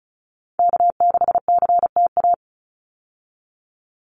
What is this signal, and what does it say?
Morse code, heard as a keyed tone, about 34 wpm, 700 Hz. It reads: K6CTA
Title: Call Signs Repeated 2x - 35wpm